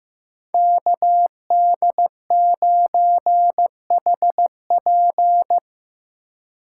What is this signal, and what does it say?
Morse code, heard as a keyed tone, about 16 wpm, 700 Hz. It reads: KD9HP
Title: USA Call Signs - 5 Characters - 3x Character Spacing - 15wpm